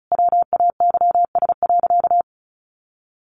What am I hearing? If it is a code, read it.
Morse code, heard as a keyed tone, about 34 wpm, 700 Hz. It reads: WAYS.
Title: A Sound Of Thunder 35wpm